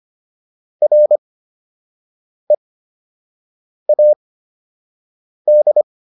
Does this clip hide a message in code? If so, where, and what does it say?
Yes: Morse code from 0.8 to 5.8 s, heard as a keyed tone, about 25 wpm, spaced as for 6 wpm, 600 Hz. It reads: READ